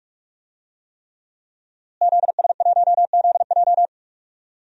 Morse code, heard as a keyed tone, about 44 wpm, 700 Hz. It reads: ZS1ZJ
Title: DXCC-100 Call Signs - 5 Characters - Repeated 3x - 45wpm